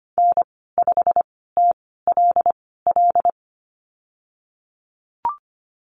Morse code, heard as a keyed tone, about 25 wpm, spaced as for 16 wpm, 700 Hz. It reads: N5TLL
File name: Lesson 15g - Number 5 - Call Signs 25wpm